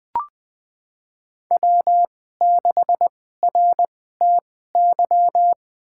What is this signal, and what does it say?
Morse code, heard as a keyed tone, about 20 wpm, spaced as for 14 wpm, 700 Hz. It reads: W6RTY
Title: USA Call Signs - 2x Character Spacing - 20wpm